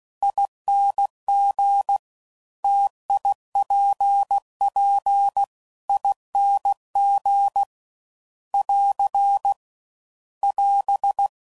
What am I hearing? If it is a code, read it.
Morse code, heard as a keyed tone, about 16 wpm, 800 Hz. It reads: ING TIPP ING <AR> <AS>
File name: Combinations / Difficult Combos / T AOEI's